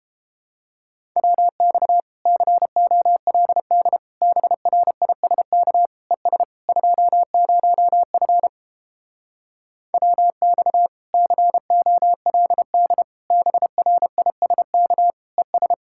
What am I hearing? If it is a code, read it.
Morse code, heard as a keyed tone, about 29 wpm, 700 Hz. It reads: WX COLD BRISK ES 20F WX COLD BRISK ES